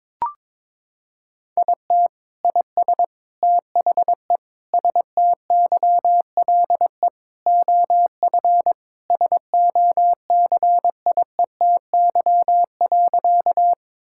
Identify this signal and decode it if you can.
Morse code, heard as a keyed tone, about 21 wpm, 700 Hz. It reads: IT IS THE STYLE OF SOCIETY.
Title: Sentences from Top 1000 Words - Review Part 5 - 22wpm